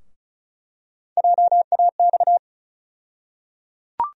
Morse code, heard as a keyed tone, about 36 wpm, 700 Hz. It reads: JAX
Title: USA Call Sign Postfixes - 3 Letters 35wpm